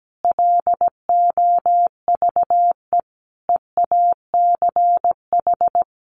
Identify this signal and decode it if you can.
Morse code, heard as a keyed tone, about 17 wpm, 700 Hz. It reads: LOVE EACH
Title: Sentences from Top 1500 Words 17wpm